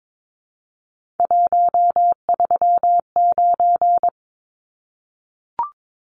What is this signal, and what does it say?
Morse code, heard as a keyed tone, about 22 wpm, 700 Hz. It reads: 139
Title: Numbers - 3-Digit 22wpm